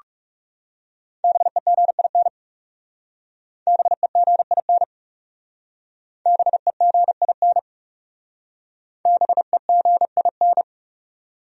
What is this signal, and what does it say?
Morse code, heard as a keyed tone, about 40 wpm, 700 Hz. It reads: BEGIN BEGIN BEGIN BEGIN